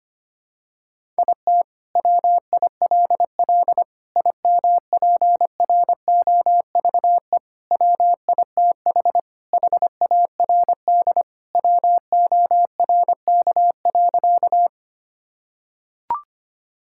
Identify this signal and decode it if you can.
Morse code, heard as a keyed tone, about 25 wpm, 700 Hz. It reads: IT WILL IMPROVE WITH HARD WORK.
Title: Sentences from Top 1000 Words - Review Part 5 - 25wpm